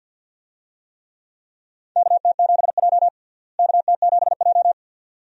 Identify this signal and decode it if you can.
Morse code, heard as a keyed tone, about 50 wpm, 700 Hz. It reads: KT7J KT7J
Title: USA Call Signs - 4 Characters - Repeated 2x - 50wpm